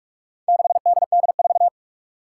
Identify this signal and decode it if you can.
Morse code, heard as a keyed tone, about 46 wpm, 700 Hz. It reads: 6DD4
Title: Four Challenging Letter-Number Combinations 45wpm